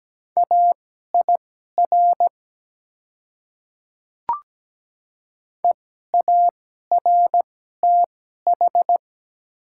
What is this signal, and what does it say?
Morse code, heard as a keyed tone, about 17 wpm, spaced as for 12 wpm, 700 Hz. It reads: AIR EARTH
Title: 2x Character Spacing - Top 300 Words - 17wpm